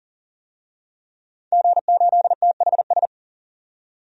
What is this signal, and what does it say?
Morse code, heard as a keyed tone, about 37 wpm, 700 Hz. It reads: G8THS
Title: DXCC-100 Call Signs - 5 Characters - 40wpm